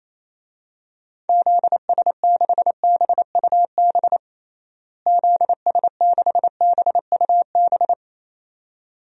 Morse code, heard as a keyed tone, about 28 wpm, 700 Hz. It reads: ZS6BUB ZS6BUB
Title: DXCC-100 Call Signs - 6 Characters - Repeated 2x - 28wpm